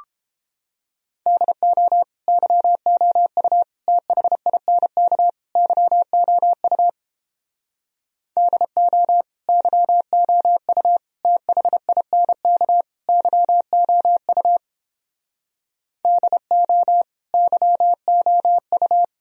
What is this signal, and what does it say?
Morse code, heard as a keyed tone, about 29 wpm, 700 Hz. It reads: DO YOU THINK YOU DO YOU THINK YOU DO YOU